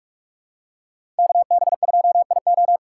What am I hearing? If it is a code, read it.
Morse code, heard as a keyed tone, about 43 wpm, 700 Hz. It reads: KB2IO
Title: USA Call Signs - 5 Characters - 4x Character Spacing - 45wpm